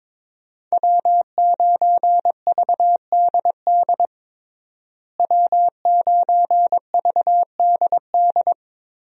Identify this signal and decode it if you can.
Morse code, heard as a keyed tone, about 22 wpm, 700 Hz. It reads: W9VDD W9VDD
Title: USA Call Signs - 5 Characters - Repeated 2x - 22wpm